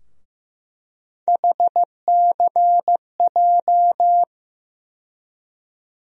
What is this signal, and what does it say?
Morse code, heard as a keyed tone, about 15 wpm, 700 Hz. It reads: HCJ
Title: USA Call Sign Postfixes - 3 Letters 15wpm